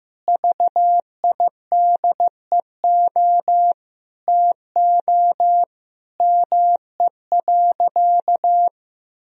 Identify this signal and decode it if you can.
Morse code, heard as a keyed tone, about 15 wpm, 700 Hz. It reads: VIDEO TO ME.
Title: Sentences from Top 1000 Words - Review Part 4 - 15wpm